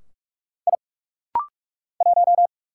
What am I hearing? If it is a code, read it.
Morse code, heard as a keyed tone, about 46 wpm, 700 Hz. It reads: I 1
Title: Single Letter-Number - Rapid-Fire 45wpm